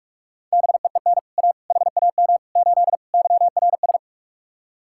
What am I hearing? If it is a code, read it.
Morse code, heard as a keyed tone, about 44 wpm, 700 Hz. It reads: BEEN A HAM 8 YRS